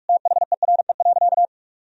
Morse code, heard as a keyed tone, about 47 wpm, 700 Hz. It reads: THERE.
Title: Sentences from Top 1000 Words - Review Part 1 - 45wpm